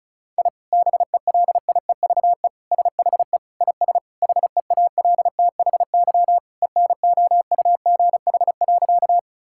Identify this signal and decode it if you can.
Morse code, heard as a keyed tone, about 36 wpm, 700 Hz. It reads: I BELIEVE SHE IS HEALTHY ENOUGH.
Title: Sentences from Top 1000 Words - Review Part 5 - 35wpm